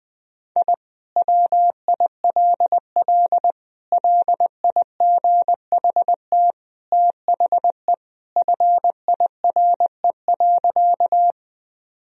Morse code, heard as a keyed tone, about 20 wpm, 700 Hz. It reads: I WILL LIGHT THE FIRE.
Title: Sentences from Top 1000 Words - Review Part 1 - 20wpm